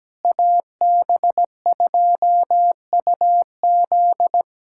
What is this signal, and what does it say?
Morse code, heard as a keyed tone, about 17 wpm, 700 Hz. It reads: AB2UZ
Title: USA Callsign DE Callsign - 5 Characters - 17wpm